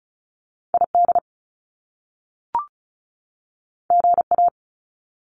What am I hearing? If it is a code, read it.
Morse code, heard as a keyed tone, about 36 wpm, 700 Hz. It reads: ID GA
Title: US States 35wpm